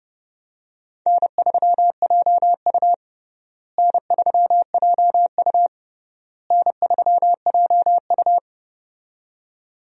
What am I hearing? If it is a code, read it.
Morse code, heard as a keyed tone, about 30 wpm, 700 Hz. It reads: N3JU N3JU N3JU
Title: USA Call Signs - 4 Characters - Repeated 3x - 30wpm